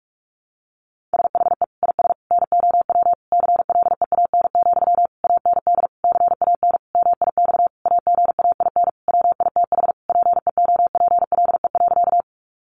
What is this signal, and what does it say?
Morse code, heard as a keyed tone, about 44 wpm, 700 Hz. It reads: SHE IS NOW CLEAN, AND CAN MIX AGAIN WITH PEOPLE.
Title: Sentences from Top 1100 Words 45wpm